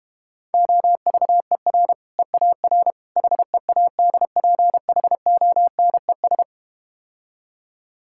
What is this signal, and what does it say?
Morse code, heard as a keyed tone, about 32 wpm, 700 Hz. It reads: OVER EAR HEADPHONES